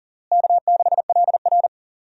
Morse code, heard as a keyed tone, about 40 wpm, 700 Hz. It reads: K6LR
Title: USA Call Signs - 4 Characters - Repeated 3x - 40wpm